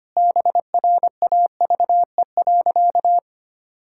Morse code, heard as a keyed tone, about 25 wpm, 700 Hz. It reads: BRAVE.
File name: practice / 25wpm / Sentences with 9-Word Phrases - 25wpm